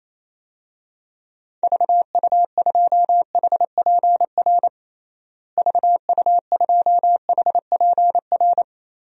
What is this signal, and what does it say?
Morse code, heard as a keyed tone, about 28 wpm, 700 Hz. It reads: VU2HPR VU2HPR
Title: DXCC-100 Call Signs - 6 Characters - Repeated 2x - 28wpm